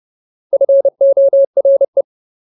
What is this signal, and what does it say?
Morse code, heard as a keyed tone, about 30 wpm, 550 Hz. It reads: FORE
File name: Ultimate Word List / 30WPM Ultimate